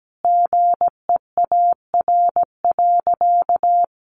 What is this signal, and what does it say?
Morse code, heard as a keyed tone, about 17 wpm, 700 Hz. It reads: GEAR.
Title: Sentences from Top 1400 Words 17wpm